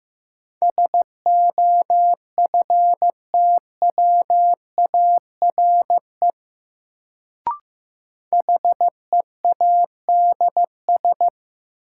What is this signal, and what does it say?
Morse code, heard as a keyed tone, about 15 wpm, 700 Hz. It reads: SOFTWARE HEADS
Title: Sentences from Top 1000 Words - Review Part 3 - New Word Drill - 15wpm